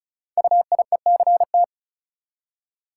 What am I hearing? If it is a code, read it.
Morse code, heard as a keyed tone, about 36 wpm, 690 Hz. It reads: UIECT
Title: Code Groups - 5 Letters - 35wpm